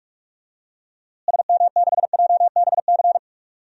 Morse code, heard as a keyed tone, about 43 wpm, 700 Hz. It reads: SM6JBC
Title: DXCC-100 Call Signs - 6 Characters - Repeated 2x - 45wpm